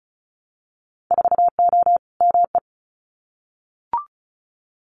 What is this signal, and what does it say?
Morse code, heard as a keyed tone, about 36 wpm, 700 Hz. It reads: 4O ME
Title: Winter Field Day Exchanges Encore 35wpm